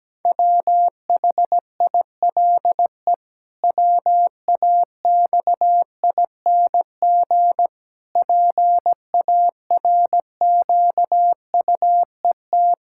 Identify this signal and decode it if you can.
Morse code, heard as a keyed tone, about 17 wpm, 700 Hz. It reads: WHILE WAXING PARQUET
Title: Pangram Challenge v2 - 17wpm